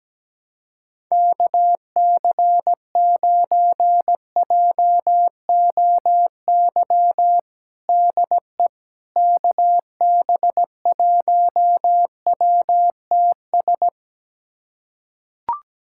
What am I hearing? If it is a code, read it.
Morse code, heard as a keyed tone, about 17 wpm, 700 Hz. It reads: KC9JOY DE KB1WTS
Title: USA Callsign DE Callsign - 6 Characters - 17wpm